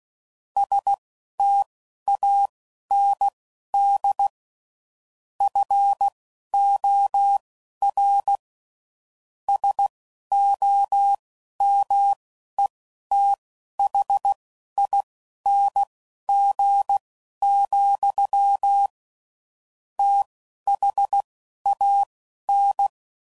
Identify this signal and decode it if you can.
Morse code, heard as a keyed tone, about 16 wpm, spaced as for 11 wpm, 800 Hz. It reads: STAND FOR SOMETHING, THAN